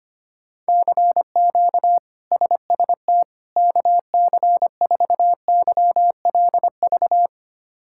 Morse code, heard as a keyed tone, about 25 wpm, 700 Hz. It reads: CQ SST KC4YLV
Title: K1USN SST QSO Practice - 25wpm